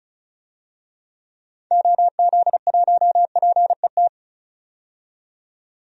Morse code, heard as a keyed tone, about 34 wpm, 700 Hz. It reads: OZ1PET
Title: DXCC-100 Call Signs - 6 Characters - 35wpm